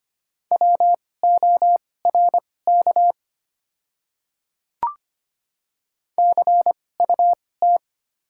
Morse code, heard as a keyed tone, about 25 wpm, spaced as for 18 wpm, 700 Hz. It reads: WORK CUT
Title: 2x Character Spacing - Top 300 Words - 25wpm